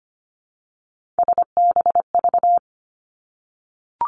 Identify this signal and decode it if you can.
Morse code, heard as a keyed tone, about 25 wpm, 700 Hz. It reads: SBV
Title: Three Challenging Letter-Number Combinations 25wpm